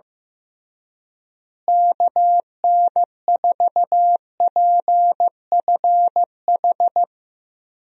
Morse code, heard as a keyed tone, about 15 wpm, 700 Hz. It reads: KN4PFH
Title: USA Call Signs - 6 Characters - Repeated 3x - 15wpm